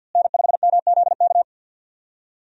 Morse code, heard as a keyed tone, about 50 wpm, 700 Hz. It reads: N5MZK
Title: USA Call Signs - 5 Characters - Repeated 2x - 50wpm